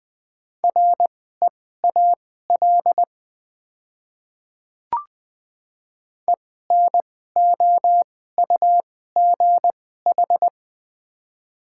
Morse code, heard as a keyed tone, about 21 wpm, spaced as for 15 wpm, 700 Hz. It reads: REAL ENOUGH
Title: 2x Character Spacing - Top 500 Words - 20wpm